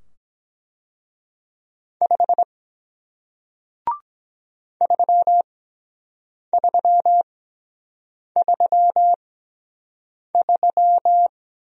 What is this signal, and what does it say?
Morse code, heard as a keyed tone, about 23 wpm, spaced as for 7 wpm, 700 Hz. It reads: 5 3333